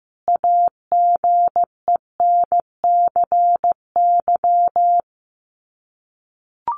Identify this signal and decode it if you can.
Morse code, heard as a keyed tone, about 15 wpm, 700 Hz. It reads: AGENCY
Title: Sets of 5 Words Encore2 15wpm